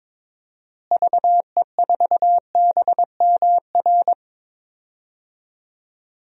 Morse code, heard as a keyed tone, about 22 wpm, 700 Hz. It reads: VE4BMR
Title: DXCC-100 Call Signs - 6 Characters - Repeated 3x - 22wpm